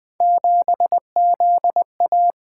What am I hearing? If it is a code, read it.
Morse code, heard as a keyed tone, about 20 wpm, 700 Hz. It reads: 7ZA
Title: Three Letter-Number Combinations - 20wpm